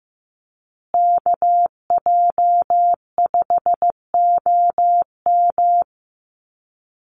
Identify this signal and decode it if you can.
Morse code, heard as a keyed tone, about 15 wpm, 700 Hz. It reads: KJ5OM
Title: USA Call Signs Repeated 2x 15wpm